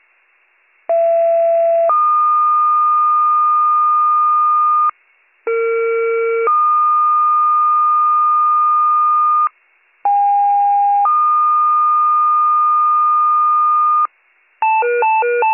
Two-tone page, heard as a one-second tone, then a three-second tone, about 670 then 1220 Hz, about 460 then 1220 Hz, about 810 then 1220 Hz. It is heard from 0.9 to 14.1 s.